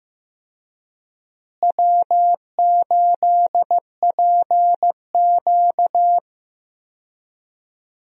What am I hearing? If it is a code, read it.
Morse code, heard as a keyed tone, about 15 wpm, 700 Hz. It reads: W8PQ
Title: USA Call Signs - 4 Characters - Repeated 2x - 15wpm